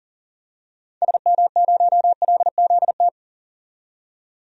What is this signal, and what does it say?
Morse code, heard as a keyed tone, about 40 wpm, 700 Hz. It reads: SM0LZT
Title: DXCC-100 Call Signs - 6 Characters - 40wpm